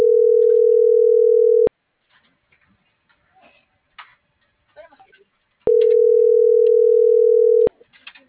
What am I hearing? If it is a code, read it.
Ringback tone (North America), 440 + 480 Hz.